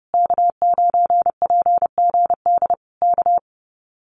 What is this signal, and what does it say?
Morse code, heard as a keyed tone, about 28 wpm, 700 Hz. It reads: K9PGD K
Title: Common QSO Phrases Encore 30wpm